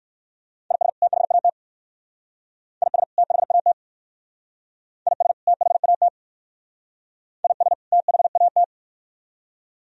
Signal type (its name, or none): Morse code